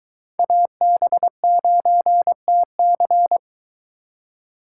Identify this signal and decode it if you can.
Morse code, heard as a keyed tone, about 23 wpm, 700 Hz. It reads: AB9TC